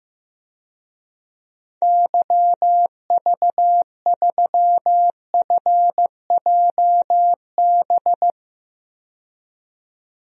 Morse code, heard as a keyed tone, about 15 wpm, 700 Hz. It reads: YV3FJB
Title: DXCC-100 Call Signs - 6 Characters - 15wpm